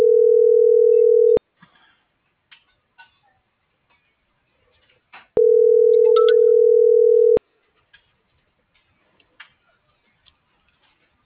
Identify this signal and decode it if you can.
Ringback tone (North America), 440 + 480 Hz.